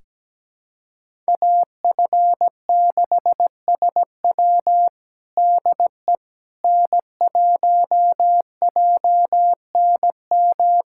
Morse code, heard as a keyed tone, about 17 wpm, 700 Hz. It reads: AF6SW DE N1JNM
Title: USA Callsign DE Callsign - 5 Characters - 17wpm